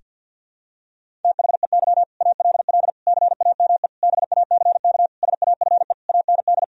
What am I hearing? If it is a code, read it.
Morse code, heard as a keyed tone, about 52 wpm, 700 Hz. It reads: THEY ALL CAME BACK SAFE AND
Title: Sentences from Top 1000 Words - Review Part 1 - 50wpm